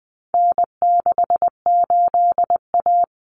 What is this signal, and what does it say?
Morse code, heard as a keyed tone, about 20 wpm, 700 Hz. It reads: N68A
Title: Common Cut Numbers - 4-Digit 20wpm